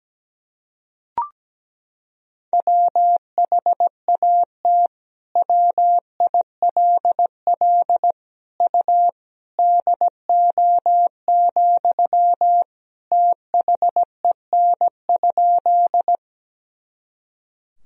Morse code, heard as a keyed tone, about 17 wpm, 700 Hz. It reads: WHAT WILL U DO, THEN?
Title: Sentences from Top 100 Words and Abbreviations 17wpm